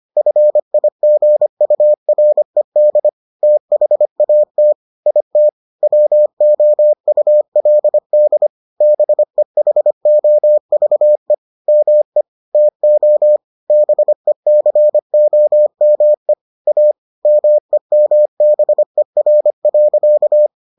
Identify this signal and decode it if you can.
Morse code, heard as a keyed tone, about 25 wpm, 600 Hz. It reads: FIGURED THAT IT WOULD BEHOVE ME TO BECOME A MEMBER.